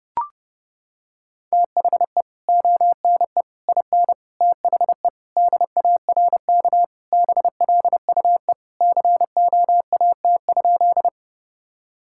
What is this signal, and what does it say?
Morse code, heard as a keyed tone, about 30 wpm, 700 Hz. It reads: THE ONE IN THE DARK BLUE COAT?
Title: Sentences from Top 1600 Words 30wpm